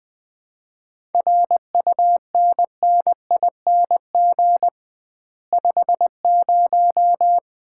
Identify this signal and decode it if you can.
Morse code, heard as a keyed tone, about 20 wpm, 700 Hz. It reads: RUNNING 50
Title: Common QSO Phrases - 2x Word-Spacing - 20wpm